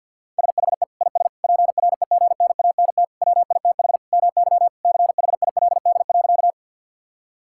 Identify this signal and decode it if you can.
Morse code, heard as a keyed tone, about 50 wpm, 700 Hz. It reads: SHE IS PREGNANT WITH MY CHILD.
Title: Sentences from Top 1000 Words - Review Part 6 - 50wpm